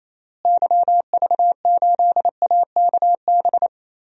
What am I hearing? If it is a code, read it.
Morse code, heard as a keyed tone, about 28 wpm, 700 Hz. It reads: YV8AKB
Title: DXCC-100 Call Signs - 6 Characters - Repeated 3x - 28wpm